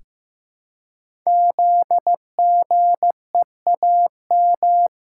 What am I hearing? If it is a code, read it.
Morse code, heard as a keyed tone, about 15 wpm, 700 Hz. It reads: ZGEAM
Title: Code Groups - 5 Letters - 15wpm